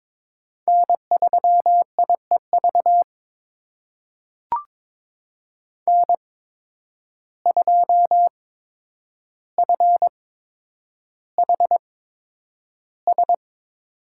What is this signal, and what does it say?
Morse code, heard as a keyed tone, about 22 wpm, spaced as for 9 wpm, 700 Hz. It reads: N3IEV N2FHS
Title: USA Call Signs - 5 Characters - 8x Character Spacing - 22wpm